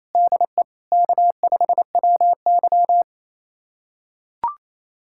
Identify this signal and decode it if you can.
Morse code, heard as a keyed tone, about 28 wpm, 700 Hz. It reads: DE K5WY
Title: USA Callsign DE Callsign - 4 Characters - 28wpm